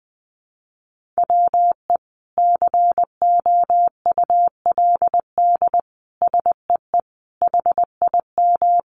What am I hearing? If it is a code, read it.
Morse code, heard as a keyed tone, about 20 wpm, 700 Hz. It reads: WE COULD SEE HIM
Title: Sentences from Top 1000 Words 20wpm